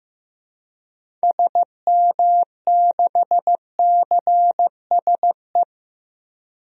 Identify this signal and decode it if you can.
Morse code, heard as a keyed tone, about 15 wpm, 700 Hz. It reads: SM6CSE
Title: DXCC-100 Call Signs - 6 Characters - Repeated 2x - 15wpm